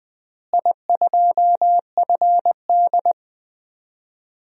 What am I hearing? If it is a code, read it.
Morse code, heard as a keyed tone, about 20 wpm, 700 Hz. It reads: I2FD
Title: DXCC-100 Call Signs - 4 Characters - Repeated 3x - 20wpm